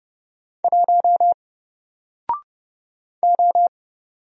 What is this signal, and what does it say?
Morse code, heard as a keyed tone, about 29 wpm, 700 Hz. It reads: 1 O
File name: Single-Letter-Number-30wpm